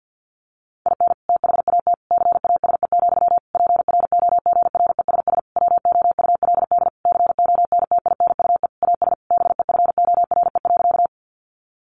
Morse code, heard as a keyed tone, about 50 wpm, 700 Hz. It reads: IN THAT CASE, PROGRESS WOULD CONTINUE AS BEFORE.
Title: Sentences from Top 1400 Words 50wpm